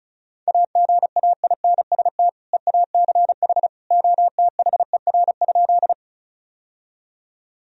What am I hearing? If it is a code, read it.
Morse code, heard as a keyed tone, about 34 wpm, 700 Hz. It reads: AGAINST EACH OTHER?
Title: Sentences from Top 1000 Words - Review Part 1 - 35wpm